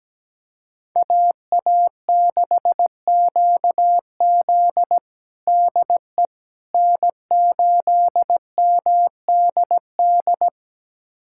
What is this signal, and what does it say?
Morse code, heard as a keyed tone, about 17 wpm, 700 Hz. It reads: AA6QZ DE N8MDD
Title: USA Callsign DE Callsign - 5 Characters - 17wpm